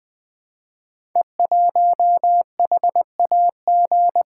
Morse code, heard as a keyed tone, about 20 wpm, 700 Hz. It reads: E1HAG
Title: Code Groups - 5 Letter:Numbers - 20wpm